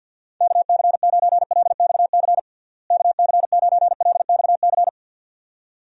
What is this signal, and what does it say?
Morse code, heard as a keyed tone, about 50 wpm, 700 Hz. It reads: KC9LXC KC9LXC
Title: USA Call Signs - 6 Characters - Repeated 2x - 50wpm